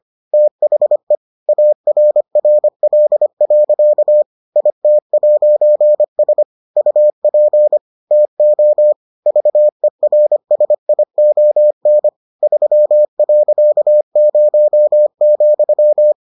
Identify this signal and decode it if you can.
Morse code, heard as a keyed tone, about 25 wpm, 600 Hz. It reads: THE ARRL. IT'S UP TO VERSION 3.0,